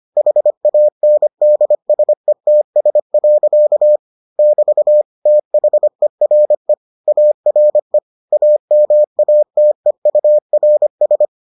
Morse code, heard as a keyed tone, about 25 wpm, 600 Hz. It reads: HANDSETS. <BT> THERE ARE AMATEURS